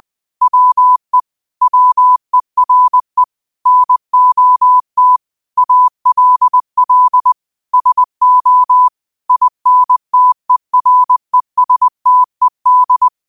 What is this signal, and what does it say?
Morse code, heard as a keyed tone, about 20 wpm, 1000 Hz. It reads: WE WERE NOT ALL SO INTERESTED